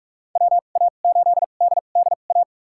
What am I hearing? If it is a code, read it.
Morse code, heard as a keyed tone, about 47 wpm, spaced as for 33 wpm, 700 Hz. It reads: WA8DDA
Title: USA Call Signs - 6 Characters - 2x Character Spacing - 45wpm